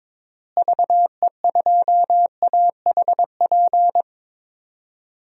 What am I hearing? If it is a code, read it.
Morse code, heard as a keyed tone, about 22 wpm, 700 Hz. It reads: VE2AHP
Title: DXCC-100 Call Signs - 6 Characters - Repeated 2x - 22wpm